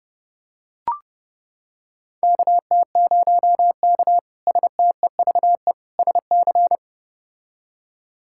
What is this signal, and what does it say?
Morse code, heard as a keyed tone, about 32 wpm, 700 Hz. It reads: KT0K STEVE SC
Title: K1USN SST QSO Practice - 30wpm